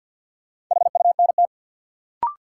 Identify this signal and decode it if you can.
Morse code, heard as a keyed tone, about 49 wpm, 700 Hz. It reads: HUNT
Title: Sentences from Top 1600 Words - New Word Drill - 50wpm